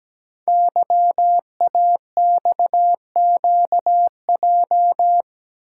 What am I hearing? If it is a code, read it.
Morse code, heard as a keyed tone, about 17 wpm, 700 Hz. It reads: YAXQJ
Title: Code Groups - 5 Letters - 17wpm